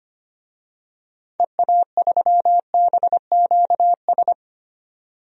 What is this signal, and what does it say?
Morse code, heard as a keyed tone, about 25 wpm, 700 Hz. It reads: EA3BQS
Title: DXCC-100 Call Signs - 6 Characters - 25wpm